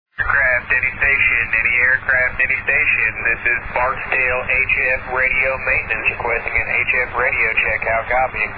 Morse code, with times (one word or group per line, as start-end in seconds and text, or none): none